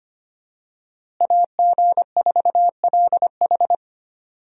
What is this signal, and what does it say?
Morse code, heard as a keyed tone, about 25 wpm, 700 Hz. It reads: AG4LH
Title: USA Call Signs - 5 Characters - Repeated 3x - 25wpm